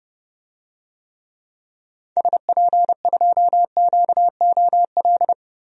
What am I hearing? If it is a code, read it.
Morse code, heard as a keyed tone, about 31 wpm, 700 Hz. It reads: SP2QOL
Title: DXCC-100 Call Signs - 6 Characters - 30wpm